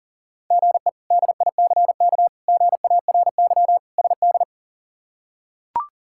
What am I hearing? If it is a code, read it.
Morse code, heard as a keyed tone, about 40 wpm, 700 Hz. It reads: GE DICK GARY SD
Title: K1USN SST QSO Practice - 40wpm